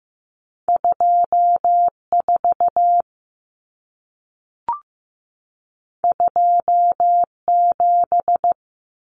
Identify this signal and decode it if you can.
Morse code, heard as a keyed tone, about 15 wpm, 700 Hz. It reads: 24 27
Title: Numbers - 2-Digit 15wpm